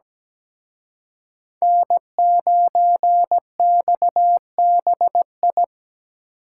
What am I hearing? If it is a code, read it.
Morse code, heard as a keyed tone, about 20 wpm, 700 Hz. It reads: N9XBI
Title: Speed Racing - USA 1x3 Call Signs 17wpm